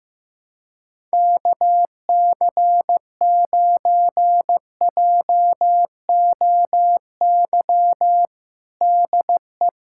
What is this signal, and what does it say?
Morse code, heard as a keyed tone, about 15 wpm, 700 Hz. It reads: KC9JOY DE
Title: USA Callsign DE Callsign - 6 Characters - 15wpm